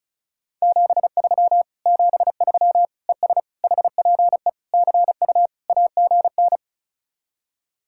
Morse code, heard as a keyed tone, about 46 wpm, 700 Hz. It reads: 73 73 ES HPE CU AGN